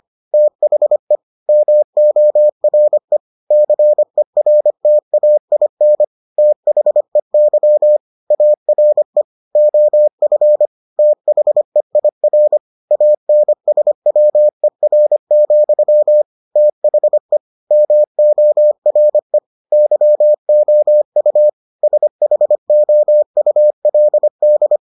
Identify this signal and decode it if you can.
Morse code, heard as a keyed tone, about 25 wpm, 600 Hz. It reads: THE MORE CERTAIN THEY ARE OF THEIR ANSWER, THE MORE YOU SHOULD